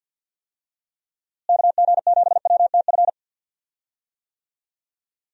Morse code, heard as a keyed tone, about 51 wpm, 700 Hz. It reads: KG7WTF
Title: USA Call Signs - 6 Characters - Repeated 2x - 50wpm